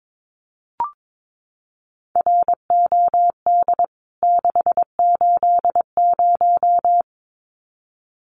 Morse code, heard as a keyed tone, about 22 wpm, 700 Hz. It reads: ROD 680
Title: CWT Exchange 22wpm